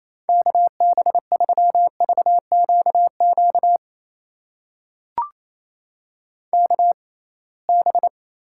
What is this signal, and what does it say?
Morse code, heard as a keyed tone, about 28 wpm, spaced as for 21 wpm, 700 Hz. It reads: KB3VQQ KB
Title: USA Call Signs - 6x Character Spacing - 28wpm